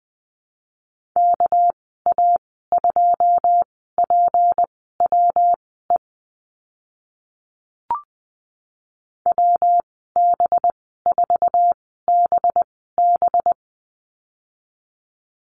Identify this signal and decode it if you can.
Morse code, heard as a keyed tone, about 20 wpm, spaced as for 14 wpm, 700 Hz. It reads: KA2PWE WB4BB
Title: 207 Lesson 33g - Letter j - Call Signs 20wpm